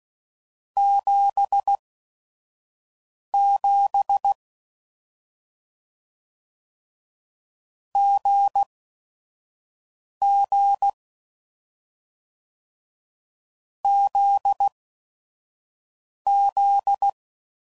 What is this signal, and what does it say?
Morse code, heard as a keyed tone, about 16 wpm, spaced as for 5 wpm, 800 Hz. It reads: 77 GG ZZ
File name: Combinations / Difficult Combos / GZ GZ7